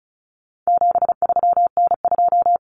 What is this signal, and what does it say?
Morse code, heard as a keyed tone, about 34 wpm, 700 Hz. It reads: 73N2
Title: Common Cut Numbers - 4-Digit 35wpm